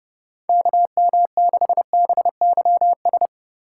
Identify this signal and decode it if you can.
Morse code, heard as a keyed tone, about 30 wpm, 700 Hz. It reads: KM6BYS
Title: USA Call Signs - 6 Characters - 4x Character Spacing - 30wpm